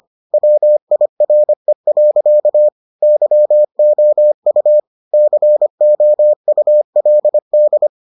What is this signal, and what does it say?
Morse code, heard as a keyed tone, about 25 wpm, 600 Hz. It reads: WIRE. YOU COULD